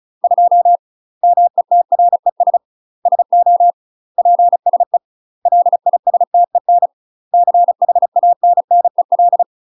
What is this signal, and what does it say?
Morse code, heard as a keyed tone, about 35 wpm, 700 Hz. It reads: 2 METRES SO PSE LISTEN CHANNEL
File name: UK MASTER QSOs TRAINING / 35WPM / UK MASTER QSOs 0022